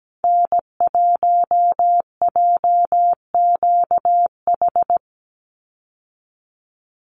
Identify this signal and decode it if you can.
Morse code, heard as a keyed tone, about 17 wpm, 700 Hz. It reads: N1JQH
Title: USA Call Signs 17wpm